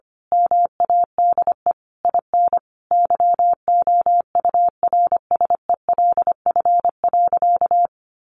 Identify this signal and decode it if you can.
Morse code, heard as a keyed tone, about 25 wpm, 700 Hz. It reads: MADE IN YOURSELF.
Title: Sentences from Top 600 Words 25wpm